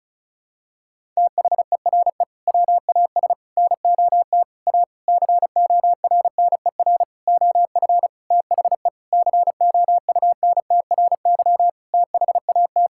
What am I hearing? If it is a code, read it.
Morse code, heard as a keyed tone, about 34 wpm, 700 Hz. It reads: THERE WAS NOT A CORNER OF THE COUNTRY THAT
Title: Sentences from Top 1600 Words 35wpm